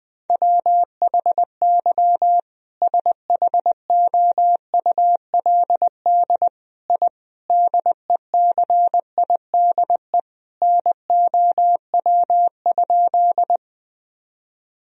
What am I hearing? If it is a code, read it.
Morse code, heard as a keyed tone, about 20 wpm, 700 Hz. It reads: WHY SHOULD I DECIDE NOW?
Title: Sentences from Top 1000 Words - Review Part 3 - 20wpm